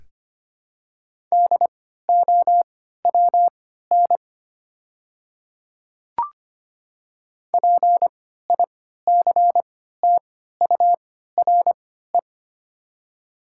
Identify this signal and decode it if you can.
Morse code, heard as a keyed tone, about 25 wpm, spaced as for 14 wpm, 700 Hz. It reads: DOWN PICTURE
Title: 3x Character Spacing - Top 200 Words - 25wpm